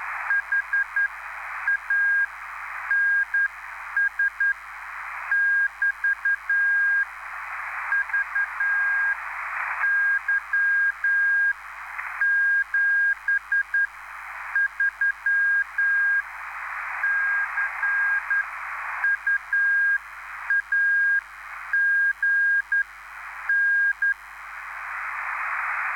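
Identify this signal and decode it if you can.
Morse code, heard as a keyed tone, about 10 wpm, spaced as for 8 wpm, 1600 Hz, with noise. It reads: HANS <BT> VY 73 CU AGN
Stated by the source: Live CW QSO between PA2BX and SP3JFK on the 14055 kHz QRS frequency, at a basic 8 wpm.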